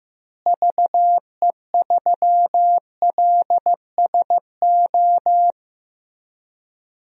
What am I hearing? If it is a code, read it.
Morse code, heard as a keyed tone, about 15 wpm, 700 Hz. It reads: VE3LSO
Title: DXCC-100 Call Signs - 6 Characters - Repeated 2x - 15wpm